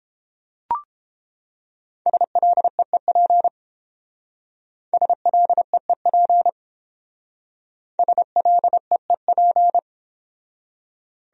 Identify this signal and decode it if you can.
Morse code, heard as a keyed tone, about 28 wpm, 700 Hz. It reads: SLEEP SLEEP SLEEP